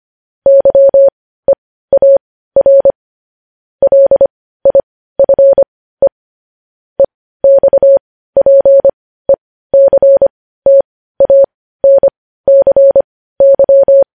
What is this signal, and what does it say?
Morse code, heard as a keyed tone, about 25 wpm, spaced as for 15 wpm, 550 Hz. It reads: YEAR LIFE EXPECTANCY
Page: news-headlines-in-morse-code-at-15-wpm